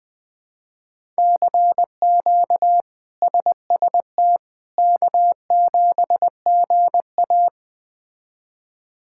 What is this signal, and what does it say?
Morse code, heard as a keyed tone, about 20 wpm, 700 Hz. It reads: CQ SST K7GA
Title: K1USN SST QSO Practice - 20wpm